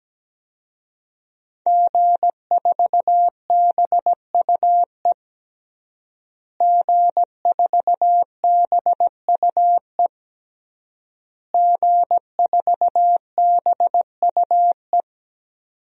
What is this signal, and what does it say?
Morse code, heard as a keyed tone, about 17 wpm, 700 Hz. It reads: G4BUE G4BUE G4BUE
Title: DXCC-100 Call Signs - 5 Characters - Repeated 3x - 17wpm